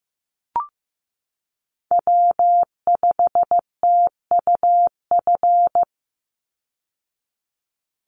Morse code, heard as a keyed tone, about 15 wpm, 700 Hz. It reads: W5TUF
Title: Call Signs Repeated 3x - 15wpm